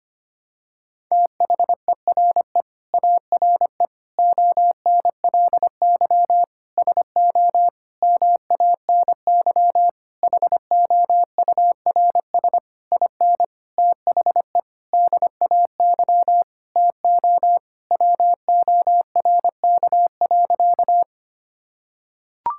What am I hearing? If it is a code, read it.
Morse code, heard as a keyed tone, about 25 wpm, 700 Hz. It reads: THERE ARE ONLY SO MANY HOURS IN THE DAY TO WORK.
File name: practice / 25wpm / Sentences with 9-Word Phrases - 25wpm